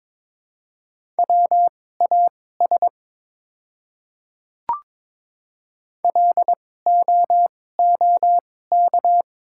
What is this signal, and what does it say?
Morse code, heard as a keyed tone, about 22 wpm, spaced as for 16 wpm, 690 Hz. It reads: WAS LOOK
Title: 2x Character Spacing - Top 100 Words - 22wpm